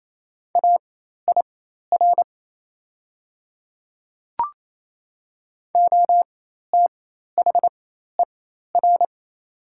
Morse code, heard as a keyed tone, about 27 wpm, spaced as for 13 wpm, 700 Hz. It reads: AIR OTHER
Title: 4x Character Spacing - Top 300 Words - 28wpm